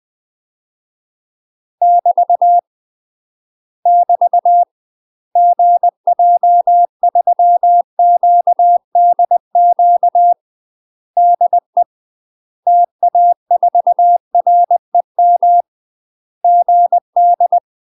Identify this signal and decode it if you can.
Morse code, heard as a keyed tone, about 20 wpm, 700 Hz. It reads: <BT> <BT> GJ3QDQ DE TA4REM GD
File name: UK MASTER QSOs TRAINING / 20WPM / UK MASTER QSOs 0011